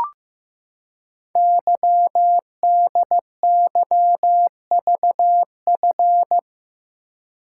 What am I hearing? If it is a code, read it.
Morse code, heard as a keyed tone, about 15 wpm, 700 Hz. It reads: YDYVF
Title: Code Groups - 5 Letters - 15wpm